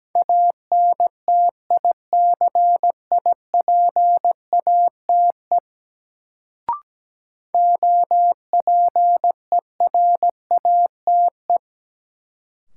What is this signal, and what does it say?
Morse code, heard as a keyed tone, about 17 wpm, 700 Hz. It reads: ANTICIPATE OPERATE
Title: Sentences from Top 1500 Words - New Word Drill Alt - 17wpm